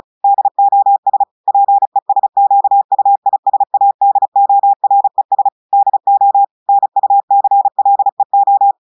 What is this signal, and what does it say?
Morse code, heard as a keyed tone, about 36 wpm, 800 Hz. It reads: DOS PESQUISADORES DO NUCLEO
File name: audios / portugais / portugais texte en clair 1 vitesse 35 mots minutes